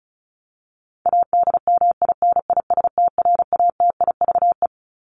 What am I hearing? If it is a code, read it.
Morse code, heard as a keyed tone, about 34 wpm, 700 Hz. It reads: ADMINISTRATIVE
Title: Top 1501-1750 Words 35wpm